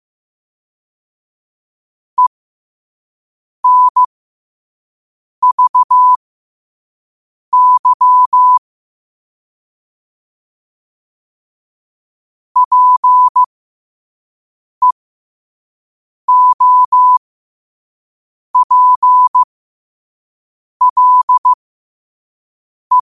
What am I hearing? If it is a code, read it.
Morse code, heard as a keyed tone, about 15 wpm, spaced as for 5 wpm, 1000 Hz. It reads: ENVY PEOPLE
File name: qotdfiles / 5wpm